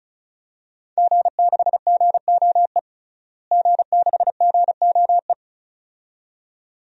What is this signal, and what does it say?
Morse code, heard as a keyed tone, about 35 wpm, 700 Hz. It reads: G6GOE G6GOE
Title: DXCC-100 Call Signs - 5 Characters - Repeated 2x - 35wpm